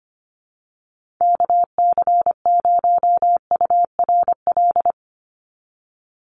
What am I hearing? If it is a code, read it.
Morse code, heard as a keyed tone, about 25 wpm, 700 Hz. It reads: KC0URL
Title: Call Signs - 25wpm